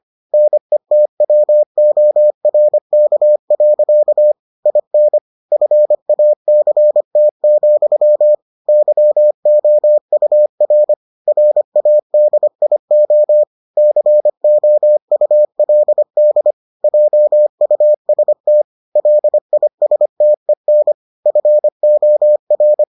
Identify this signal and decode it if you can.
Morse code, heard as a keyed tone, about 25 wpm, 600 Hz. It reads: NETWORK. IN FACT, YOUR RADIO COULD JUST LISTEN FOR